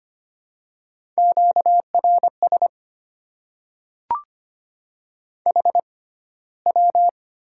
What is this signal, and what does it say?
Morse code, heard as a keyed tone, about 25 wpm, spaced as for 15 wpm, 700 Hz. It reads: QRS HW
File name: Top 90 QSO Elements - 6x Character Spacing - 25wpm